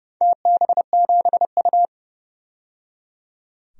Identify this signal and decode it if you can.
Morse code, heard as a keyed tone, about 30 wpm, 700 Hz. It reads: TB7U
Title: DXCC-100 Call Signs - 4 Characters - 30wpm